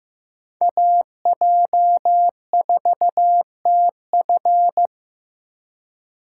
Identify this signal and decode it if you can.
Morse code, heard as a keyed tone, about 15 wpm, 700 Hz. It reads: AJ4TF
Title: K1USN SST QSO Practice - 15wpm